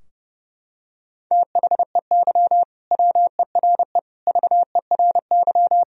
Morse code, heard as a keyed tone, about 30 wpm, 700 Hz. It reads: THEY WERE VERY
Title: Sentences from Top 1000 Words - Review Part 3 - 30wpm